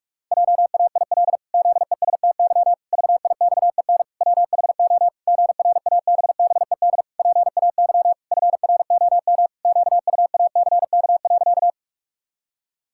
Morse code, heard as a keyed tone, about 46 wpm, 700 Hz. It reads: JAIL ZESTY VIXEN WHO GRABBED PAY FROM QUACK.
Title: Pangram Challenge v2 - 45wpm